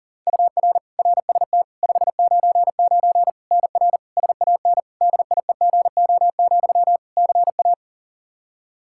Morse code, heard as a keyed tone, about 40 wpm, 700 Hz. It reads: UR RST 599 NR SAN DIEGO, CA
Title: Practice QSO - Exchange 1 - Beginning - 40wpm